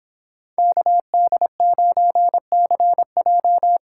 Morse code, heard as a keyed tone, about 26 wpm, 700 Hz. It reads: KD9CJ